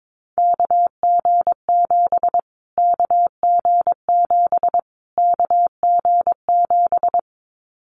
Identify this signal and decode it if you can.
Morse code, heard as a keyed tone, about 22 wpm, 700 Hz. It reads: KG7 KG7 KG7
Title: Callsign Prefixes for Top 25 DXCC - Repeated 3x - 22wpm